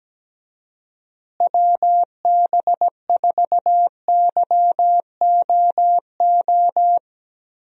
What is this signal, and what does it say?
Morse code, heard as a keyed tone, about 17 wpm, 700 Hz. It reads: WB4YOO
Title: USA Call Signs - 6 Characters - Repeated 3x - 17wpm